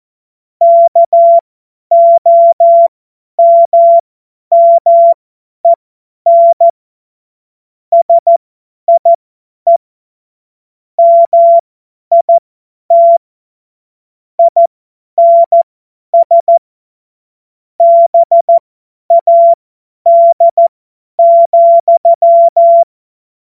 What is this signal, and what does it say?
Morse code, heard as a keyed tone, about 14 wpm, spaced as for 10 wpm, 680 Hz. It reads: KOMMEN SIE MIT INS BAD,
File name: Morsekurs / Witze / Witz 23 Januar 2024 50 F70